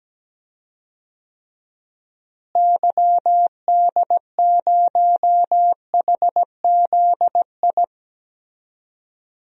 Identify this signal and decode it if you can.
Morse code, heard as a keyed tone, about 17 wpm, 700 Hz. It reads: YD0HZI
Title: DXCC-100 Call Signs - 6 Characters - Repeated 3x - 17wpm